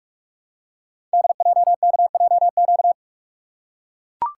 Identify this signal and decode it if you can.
Morse code, heard as a keyed tone, about 44 wpm, 700 Hz. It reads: DJKJQ
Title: Code Groups - 5 Letters - 45wpm